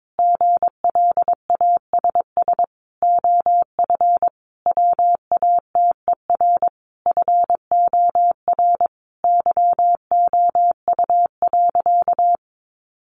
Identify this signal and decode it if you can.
Morse code, heard as a keyed tone, about 22 wpm, 700 Hz. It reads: GLASS OF WATER FOR YOU.
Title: Sentences from Top 1300 Words 22wpm